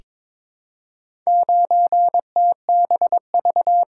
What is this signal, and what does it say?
Morse code, heard as a keyed tone, about 22 wpm, 700 Hz. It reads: 9TBV
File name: practice / 22wpm / Four Letter-Number Combinations - 22wpm